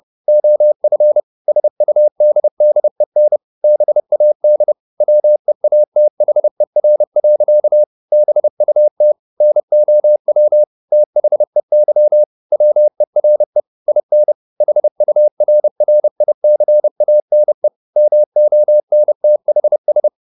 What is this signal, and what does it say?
Morse code, heard as a keyed tone, about 30 wpm, 600 Hz. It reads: OF SUDDEN BAD WEATHER. BUT NOW THEY WERE IN HURRICANE MONTHS
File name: the old man and the sea / the old man and the sea 1 30wpm